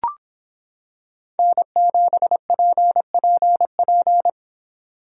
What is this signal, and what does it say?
Morse code, heard as a keyed tone, about 26 wpm, 700 Hz. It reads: N7PPP